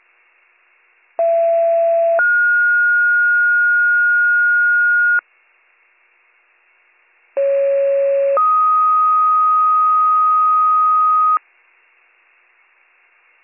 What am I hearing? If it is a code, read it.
Two-tone page, heard as a one-second tone, then a three-second tone, about 670 then 1430 Hz, about 550 then 1250 Hz. It is heard from 1.2 to 11.4 s.